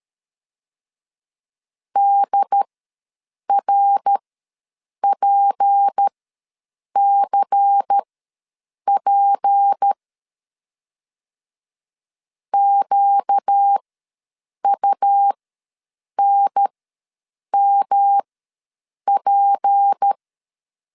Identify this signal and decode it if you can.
Morse code, heard as a keyed tone, about 13 wpm, spaced as for 7 wpm, 800 Hz. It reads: DRPCP QUNMP
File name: LEZIONI / 6 PKX0 / 6 P ws 12wpm 3117